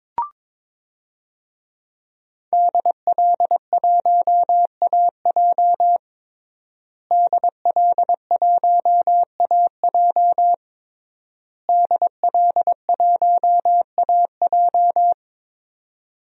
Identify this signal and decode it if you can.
Morse code, heard as a keyed tone, about 22 wpm, 700 Hz. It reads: DL1AJ DL1AJ DL1AJ
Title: DXCC-100 Call Signs - 5 Characters - Repeated 3x - 22wpm